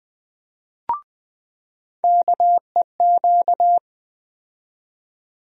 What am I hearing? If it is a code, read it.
Morse code, heard as a keyed tone, about 20 wpm, 700 Hz. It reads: KEQ
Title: USA Call Sign Postfixes - 3 Letters 20wpm